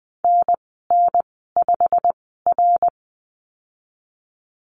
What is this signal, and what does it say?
Morse code, heard as a keyed tone, about 20 wpm, spaced as for 14 wpm, 700 Hz. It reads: NN5R
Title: 088 Lesson 15g - Number 5 - Call Signs 20wpm